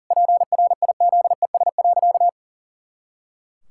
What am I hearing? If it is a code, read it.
Morse code, heard as a keyed tone, about 37 wpm, 700 Hz. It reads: PRIZES.
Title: Pangram Challenge v2 - 40wpm